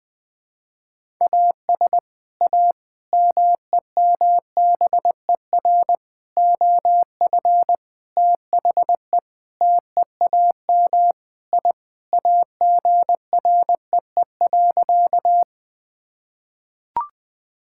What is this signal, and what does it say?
Morse code, heard as a keyed tone, about 20 wpm, 700 Hz. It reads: AS A MEMBER OF THE TEAM I AGREE.
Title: Sentences with 5-Word Phrases - 20wpm